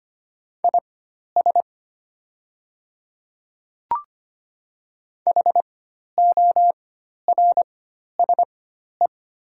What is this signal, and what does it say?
Morse code, heard as a keyed tone, about 25 wpm, spaced as for 12 wpm, 700 Hz. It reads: IS HORSE
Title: 4x Character Spacing - Top 300 Words - 25wpm